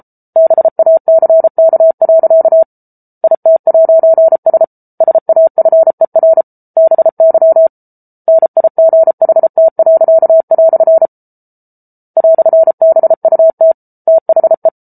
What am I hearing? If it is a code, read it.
Morse code, heard as a keyed tone, about 34 wpm, 650 Hz. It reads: BACK. IT'S SAFER BY NIGHT." "BUT THE